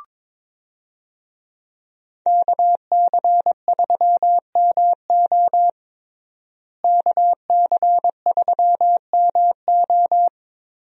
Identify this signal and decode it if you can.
Morse code, heard as a keyed tone, about 22 wpm, 700 Hz. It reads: KC3MO KC3MO
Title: USA Call Signs - 5 Characters - Repeated 3x - 22wpm